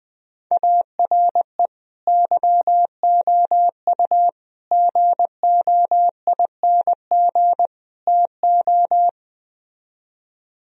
Morse code, heard as a keyed tone, about 20 wpm, 700 Hz. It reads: ARE YOU GOING TO
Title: Speed-Racing - Sets of 4 Words 17wpm